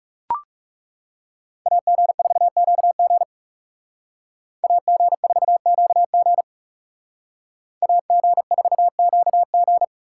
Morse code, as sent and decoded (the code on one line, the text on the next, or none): .- --. ....- --.- --. / .- --. ....- --.- --. / .- --. ....- --.- --.
AG4QG AG4QG AG4QG